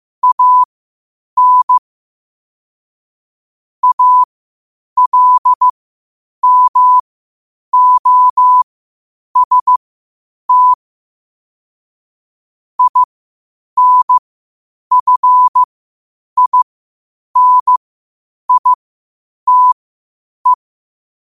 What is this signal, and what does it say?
Morse code, heard as a keyed tone, about 15 wpm, spaced as for 8 wpm, 1000 Hz. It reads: AN ALMOST INFINITE
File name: qotdfiles / 8wpm